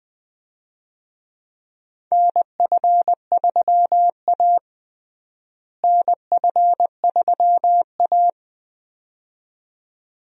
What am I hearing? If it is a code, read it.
Morse code, heard as a keyed tone, about 20 wpm, 700 Hz. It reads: NF3A NF3A
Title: USA Call Signs - 4 Characters - Repeated 2x - 20wpm